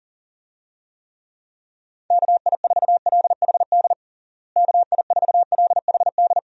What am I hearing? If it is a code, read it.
Morse code, heard as a keyed tone, about 40 wpm, 700 Hz. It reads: KI4LHD KI4LHD
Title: USA Call Signs - 6 Characters - Repeated 2x - 40wpm